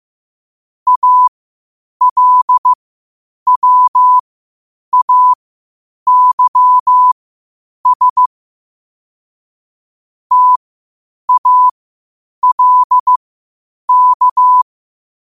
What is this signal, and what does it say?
Morse code, heard as a keyed tone, about 15 wpm, spaced as for 8 wpm, 1000 Hz. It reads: ALWAYS TALK
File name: qotdfiles / 8wpm